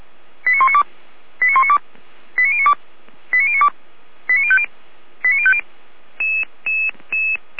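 Five-tone selective call (ZVEI), heard as quick runs of five tones, tones of about 80 ms. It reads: 89192 89192 89092 89092 89050 89050
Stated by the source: ZVEI five-tone call 89160 received via SDR, on the Feuerwehr Orlamuende audio server.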